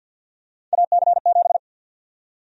Morse code, heard as a keyed tone, about 50 wpm, 700 Hz. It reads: AC7
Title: USA Call Sign Prefixes - 50wpm